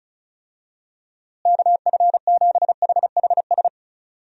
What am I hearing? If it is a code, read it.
Morse code, heard as a keyed tone, about 34 wpm, 700 Hz. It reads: KF7HHS
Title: USA Call Signs - 6 Characters - Repeated 2x - 35wpm